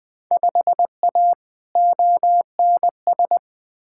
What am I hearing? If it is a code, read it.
Morse code, heard as a keyed tone, about 20 wpm, 700 Hz. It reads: 5A ONS
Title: Field Day Exchanges Encore - 20wpm